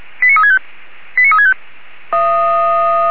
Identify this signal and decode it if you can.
Five-tone selective call (ZVEI), heard as quick runs of five tones, tones of about 70 ms. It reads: 89365 89365
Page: /464/SDR/ZVEI/